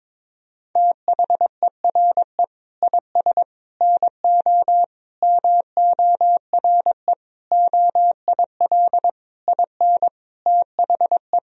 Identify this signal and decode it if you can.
Morse code, heard as a keyed tone, about 22 wpm, 700 Hz. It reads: THERE IS NO MORE OIL IN THE
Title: Sentences from Top 1000 Words - Review Part 4 - 22wpm